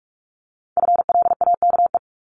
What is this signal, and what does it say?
Morse code, heard as a keyed tone, about 46 wpm, 700 Hz. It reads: FLAKE
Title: Multi-letter Phonemes 45wpm